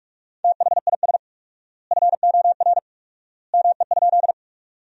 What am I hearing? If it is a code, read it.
Morse code, heard as a keyed tone, about 46 wpm, 700 Hz. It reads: THIS FOR ME?
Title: 4x Word-Spacing - Sentences from Top 200 Words - 45wpm